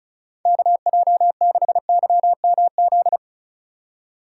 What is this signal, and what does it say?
Morse code, heard as a keyed tone, about 36 wpm, 700 Hz. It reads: KJ6YMZ
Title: USA Call Signs - 6 Characters - 3x Character Spacing - 35wpm